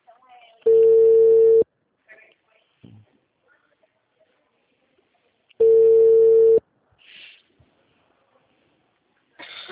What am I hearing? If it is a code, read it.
Ringback tone (China, 450 Hz), with speech.